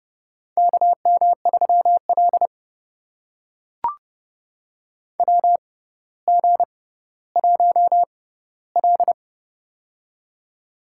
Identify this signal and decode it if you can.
Morse code, heard as a keyed tone, about 30 wpm, spaced as for 15 wpm, 700 Hz. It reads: KM3L WG1L
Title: USA Call Signs - 4 Characters - 6x Character Spacing - 30wpm